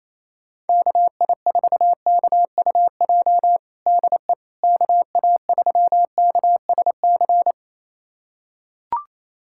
Morse code, heard as a keyed tone, about 28 wpm, 700 Hz. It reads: KI4KUJ DE KA3KSC
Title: USA Callsign DE Callsign 28wpm